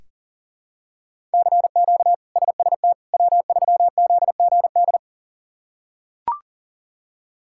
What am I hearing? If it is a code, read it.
Morse code, heard as a keyed tone, about 37 wpm, 700 Hz. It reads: CQ SST W3ZGD
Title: K1USN SST QSO Practice - 40wpm